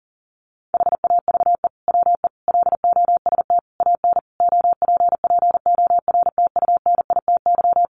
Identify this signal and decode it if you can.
Morse code, heard as a keyed tone, about 40 wpm, 700 Hz. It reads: HAVE WE LOST AN OPPORTUNITY
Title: Sentences from Top 1400 Words 40wpm